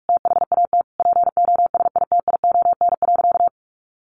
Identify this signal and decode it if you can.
Morse code, heard as a keyed tone, about 44 wpm, 700 Hz. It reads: THAT POSITION.
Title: Sentences from Top 900 Words 45wpm